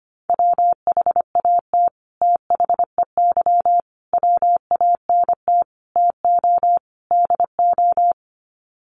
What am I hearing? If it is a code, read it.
Morse code, heard as a keyed tone, about 25 wpm, 700 Hz. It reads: WHAT THEY WANT TO DO
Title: Sets of 5 Words Encore2 25wpm